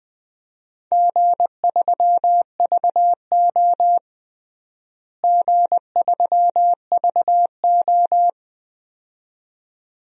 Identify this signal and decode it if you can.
Morse code, heard as a keyed tone, about 20 wpm, 700 Hz. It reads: G3VO G3VO
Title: DXCC-100 Call Signs - 4 Characters - Repeated 2x - 20wpm